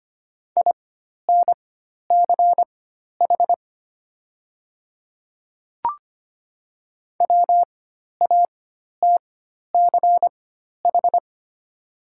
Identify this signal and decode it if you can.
Morse code, heard as a keyed tone, about 25 wpm, spaced as for 12 wpm, 700 Hz. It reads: INCH WATCH
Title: 4x Character Spacing - Top 500 Words - 25wpm